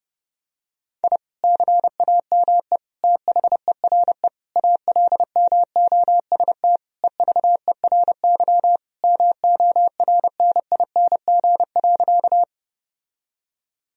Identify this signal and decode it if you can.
Morse code, heard as a keyed tone, about 32 wpm, 700 Hz. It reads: I CAME THERE ALMOST EVERY MORNING.
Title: Sentences from Top 1000 Words - Review Part 4 - 30wpm